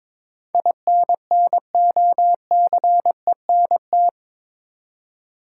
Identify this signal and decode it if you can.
Morse code, heard as a keyed tone, about 22 wpm, 700 Hz. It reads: INNOCENT
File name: practice / 22wpm / Eight Letter Words - 22wpm